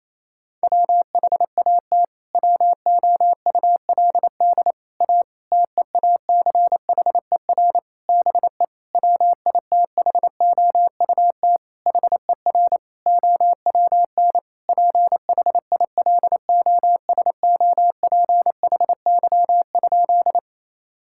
Morse code, heard as a keyed tone, about 28 wpm, 700 Hz. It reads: WHAT WOULD A TEACHER BE WITHOUT HER OWN PHILOSOPHY?
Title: Sentences from Top 1000 Words - Review Part 6 - 28wpm